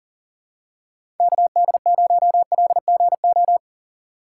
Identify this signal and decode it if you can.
Morse code, heard as a keyed tone, about 40 wpm, 700 Hz. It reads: KD0LGO
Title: USA Call Signs - 6 Characters - 2x Character Spacing - 40wpm